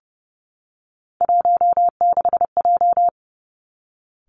Morse code, heard as a keyed tone, about 30 wpm, 700 Hz. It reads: 16J
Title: Three Challenging Letter-Number Combinations 30wpm